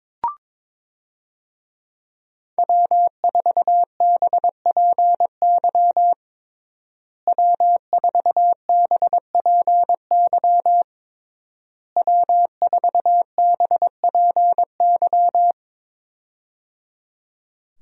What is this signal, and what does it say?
Morse code, heard as a keyed tone, about 22 wpm, 700 Hz. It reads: W4BPY W4BPY W4BPY
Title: USA Call Signs - 5 Characters - Repeated 3x - 22wpm